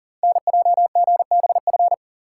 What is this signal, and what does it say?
Morse code, heard as a keyed tone, about 38 wpm, 700 Hz. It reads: NJGBF
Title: Code Groups - 5 Letters - 40wpm